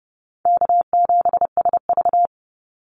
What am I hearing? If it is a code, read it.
Morse code, heard as a keyed tone, about 30 wpm, 700 Hz. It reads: K7SV
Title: Easy Call Signs - 30wpm